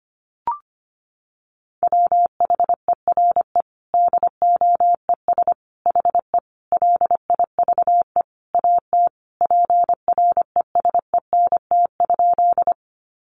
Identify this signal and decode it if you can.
Morse code, heard as a keyed tone, about 25 wpm, 700 Hz. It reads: WHERE DOES HE LIVE AT PRESENT?
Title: Sentences from Top 800 Words 25wpm